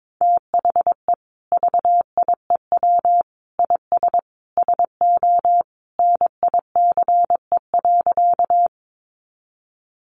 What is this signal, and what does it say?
Morse code, heard as a keyed tone, about 22 wpm, 700 Hz. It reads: THE VIEW IS SO NICE.
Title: Additional Easy Phrases 22wpm